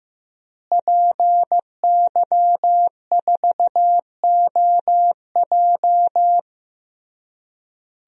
Morse code, heard as a keyed tone, about 15 wpm, 700 Hz. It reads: PY4OJ
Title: DXCC-100 Call Signs - 5 Characters - Repeated 3x - 15wpm